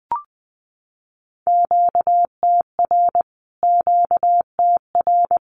Morse code, heard as a keyed tone, about 20 wpm, 700 Hz. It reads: QTR QTR
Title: Q-Codes Repeated 3x 20wpm